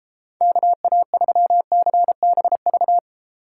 Morse code, heard as a keyed tone, about 35 wpm, 700 Hz. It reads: KA3CBV
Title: Speed Racing - USA 2x3 Call Signs 28wpm